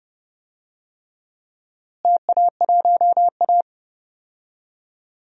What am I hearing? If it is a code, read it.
Morse code, heard as a keyed tone, about 30 wpm, 700 Hz. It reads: TA1A
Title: DXCC-100 Call Signs - 4 Characters - Repeated 3x - 30wpm